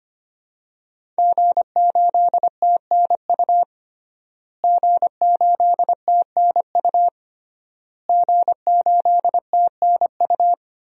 Morse code, heard as a keyed tone, about 25 wpm, 700 Hz. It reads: G8TNU G8TNU G8TNU
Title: DXCC-100 Call Signs - 5 Characters - Repeated 3x - 25wpm